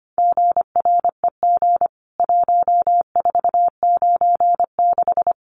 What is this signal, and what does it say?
Morse code, heard as a keyed tone, about 25 wpm, 700 Hz. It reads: GREG 1496
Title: CWT Exchange 25wpm